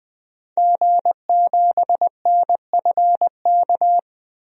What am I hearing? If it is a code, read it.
Morse code, heard as a keyed tone, about 20 wpm, 700 Hz. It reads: G7NFK
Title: DXCC-100 Call Signs - 5 Characters - Repeated 3x - 20wpm